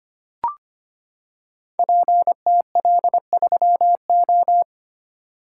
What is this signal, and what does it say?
Morse code, heard as a keyed tone, about 25 wpm, 700 Hz. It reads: PTL3O
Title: Code Groups - 5 Letter:Numbers - 25wpm